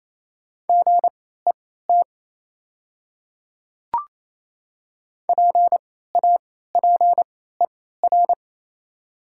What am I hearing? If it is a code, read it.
Morse code, heard as a keyed tone, about 28 wpm, spaced as for 16 wpm, 700 Hz. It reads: GET PAPER
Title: 3x Character Spacing - Top 300 Words - 28wpm